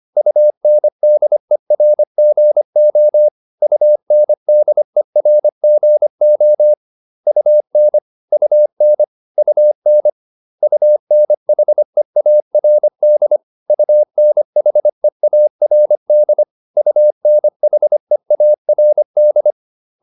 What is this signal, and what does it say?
Morse code, heard as a keyed tone, about 25 wpm, 600 Hz. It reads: UNDERGO UNDERGO UN UN UN UNHEARD UNHEARD UNHEARD